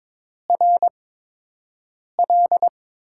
Morse code, heard as a keyed tone, about 22 wpm, spaced as for 6 wpm, 700 Hz. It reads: RL
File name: Top 20 QSO Elements - 8x Character Spacing - 22wpm